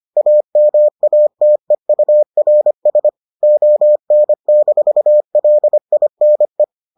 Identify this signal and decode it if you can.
Morse code, heard as a keyed tone, about 25 wpm, 600 Hz. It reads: AMATEURS ON-LINE